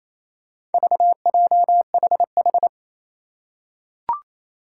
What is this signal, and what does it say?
Morse code, heard as a keyed tone, about 29 wpm, 700 Hz. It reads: VJHH
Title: Four Challenging Letter-Number Combinations 28wpm